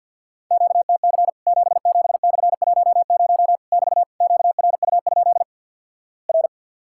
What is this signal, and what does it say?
Morse code, heard as a keyed tone, about 50 wpm, 700 Hz. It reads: QTC 77/10 <BT> QRU? R
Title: Worked All Europe DX Contest - QTC Exchange - 50wpm